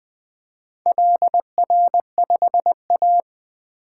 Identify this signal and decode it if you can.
Morse code, heard as a keyed tone, about 23 wpm, 700 Hz. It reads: LR5A
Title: Speed-Racing - Four Letter-Number Combinations - 20wpm